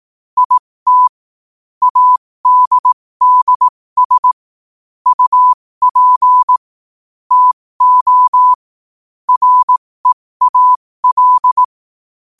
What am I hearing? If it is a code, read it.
Morse code, heard as a keyed tone, about 18 wpm, spaced as for 15 wpm, 1000 Hz. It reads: IT ADDS UP TO REAL
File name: qotdfiles / 15wpm